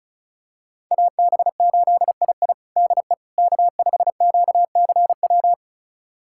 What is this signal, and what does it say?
Morse code, heard as a keyed tone, about 36 wpm, 700 Hz. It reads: AB8II DE K5QCW
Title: USA Callsign DE Callsign - 5 Characters - 35wpm